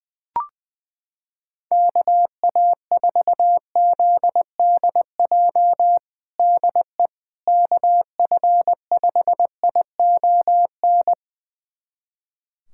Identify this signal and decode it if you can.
Morse code, heard as a keyed tone, about 21 wpm, 700 Hz. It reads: KA4ZDJ DE KF5ION
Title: USA Callsign DE Callsign - 6 Characters - 20wpm